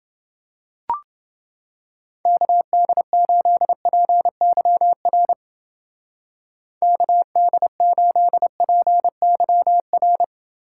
Morse code, heard as a keyed tone, about 28 wpm, 700 Hz. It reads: KD8PYR KD8PYR